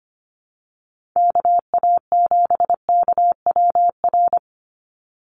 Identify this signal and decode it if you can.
Morse code, heard as a keyed tone, about 25 wpm, 700 Hz. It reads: KA7KWR
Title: USA Call Signs 25wpm